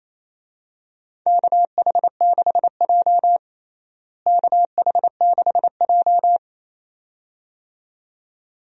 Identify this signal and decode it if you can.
Morse code, heard as a keyed tone, about 28 wpm, 700 Hz. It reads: KH6J KH6J
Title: DXCC-100 Call Signs - 4 Characters - Repeated 2x - 28wpm